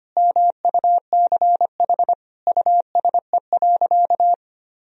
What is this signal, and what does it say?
Morse code, heard as a keyed tone, about 25 wpm, 700 Hz. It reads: MUCH USE.
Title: Sentences from Top 1000 Words - Review Part 3 - 25wpm